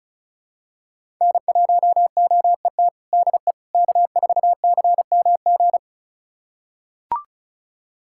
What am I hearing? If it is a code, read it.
Morse code, heard as a keyed tone, about 34 wpm, 700 Hz. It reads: N1OET DE K4CMG
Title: USA Callsign DE Callsign - 5 Characters - 35wpm